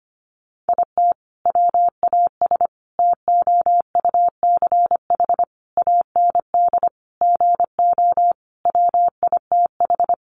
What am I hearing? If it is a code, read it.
Morse code, heard as a keyed tone, about 25 wpm, 700 Hz. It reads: IT WAS TOUCH AND GO WITH
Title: Sentences from Top 1200 Words 25wpm